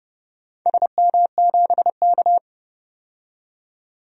Morse code, heard as a keyed tone, about 31 wpm, 700 Hz. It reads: SM7K
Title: DXCC-100 Call Signs - 4 Characters - 30wpm